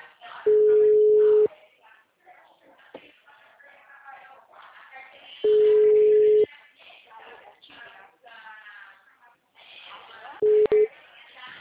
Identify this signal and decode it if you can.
Ringback tone (Europe, 425 Hz), with speech.